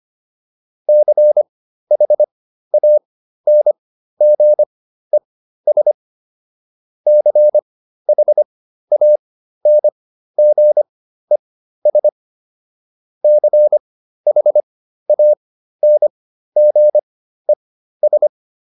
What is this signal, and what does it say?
Morse code, heard as a keyed tone, about 25 wpm, spaced as for 13 wpm, 600 Hz. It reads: CHANGES CHANGES CHANGES